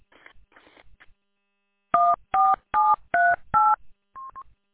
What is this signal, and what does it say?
DTMF, heard as telephone touch tones, tones of about 220 ms. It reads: 14738
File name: w8cmn- -combined